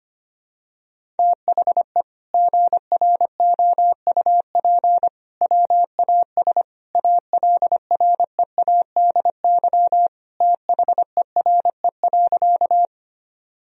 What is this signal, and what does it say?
Morse code, heard as a keyed tone, about 25 wpm, 700 Hz. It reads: THE GROUP WAS ALREADY THERE.
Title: Sentences from Top 1000 Words - Review Part 1 - 25wpm